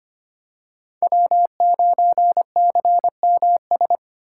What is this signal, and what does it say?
Morse code, heard as a keyed tone, about 25 wpm, 700 Hz. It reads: W9CMS
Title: USA Call Signs - 5 Characters - 3x Character Spacing - 25wpm